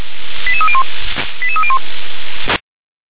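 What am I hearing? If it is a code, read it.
Five-tone selective call (ZVEI), heard as quick runs of five tones, tones of about 70 ms. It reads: 88301 88301
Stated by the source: ZVEI five-tone selective call 88994, received on SDR.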